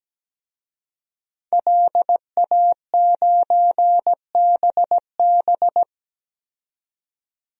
Morse code, heard as a keyed tone, about 17 wpm, 700 Hz. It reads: LA9BB
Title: DXCC-100 Call Signs - 5 Characters - 17wpm